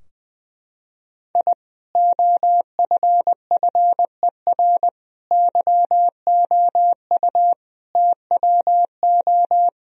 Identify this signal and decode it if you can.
Morse code, heard as a keyed tone, about 21 wpm, 700 Hz. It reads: I OFFER YOU TWO
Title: Sentences from Top 1000 Words - Review Part 4 - 20wpm